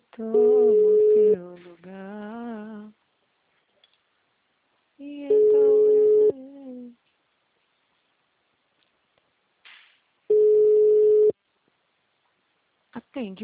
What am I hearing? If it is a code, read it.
Ringback tone (Europe, 425 Hz), with speech.